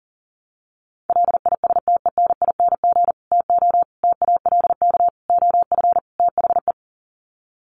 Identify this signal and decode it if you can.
Morse code, heard as a keyed tone, about 40 wpm, 700 Hz. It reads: LISTENING TO TALK OF THE
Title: Sets of 5 Words 40wpm